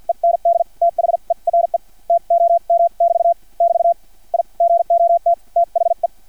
Morse code, heard as a keyed tone, about 50 wpm, spaced as for 41 wpm, 680 Hz. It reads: END THERE TOM <BT> <BT> I GOT THE
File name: archived / CRSnet